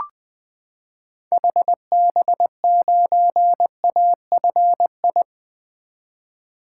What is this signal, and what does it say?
Morse code, heard as a keyed tone, about 21 wpm, 700 Hz. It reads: HB9AFI
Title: 2023 SOTA Activator Call Signs - 20wpm